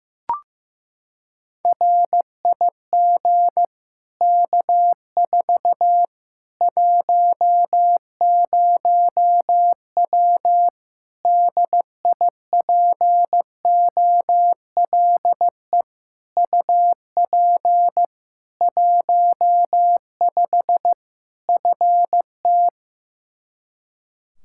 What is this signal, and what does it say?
Morse code, heard as a keyed tone, about 15 wpm, 700 Hz. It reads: RIG K4 10W DIPOLE UP 15 FT
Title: Practice QSO - Exchange 2 - Info - 15wpm